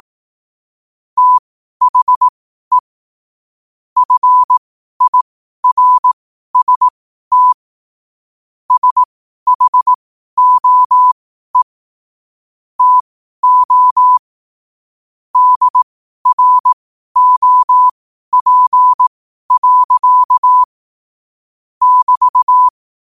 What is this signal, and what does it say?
Morse code, heard as a keyed tone, about 18 wpm, spaced as for 12 wpm, 1000 Hz. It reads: THE FIRST SHOE TO DROP. <BT>
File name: qotdfiles / 12wpm